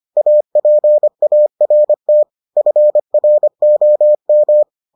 Morse code, heard as a keyed tone, about 25 wpm, 600 Hz. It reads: APART FROM